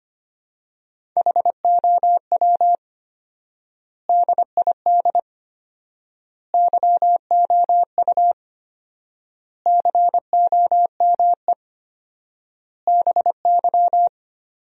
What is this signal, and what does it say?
Morse code, heard as a keyed tone, about 25 wpm, 700 Hz. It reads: HOW DID YOU COME BY
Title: 4x Word-Spacing - Sentences from Top 100 Words - 25wpm